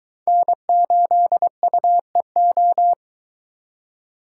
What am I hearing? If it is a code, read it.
Morse code, heard as a keyed tone, about 22 wpm, 700 Hz. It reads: N8UEO